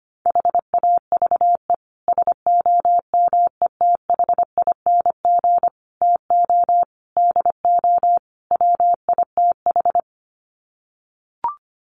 Morse code, heard as a keyed tone, about 25 wpm, 690 Hz. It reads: HAVE SOMETHING TO DO WITH
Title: Sets of 5 Words Encore 25wpm